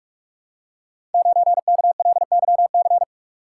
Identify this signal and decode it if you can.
Morse code, heard as a keyed tone, about 44 wpm, 700 Hz. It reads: 9KLYC
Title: Code Groups - 5 Letter:Numbers - 45wpm